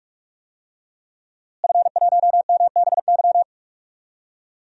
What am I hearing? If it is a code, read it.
Morse code, heard as a keyed tone, about 46 wpm, 700 Hz. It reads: F1MBY
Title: DXCC-100 Call Signs - 5 Characters - Repeated 3x - 45wpm